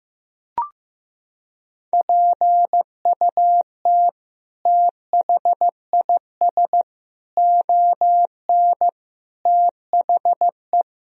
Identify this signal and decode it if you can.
Morse code, heard as a keyed tone, about 15 wpm, 700 Hz. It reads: PUT THIS ON THE
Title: Sentences from Top 1000 Words - Review Part 1 - 15wpm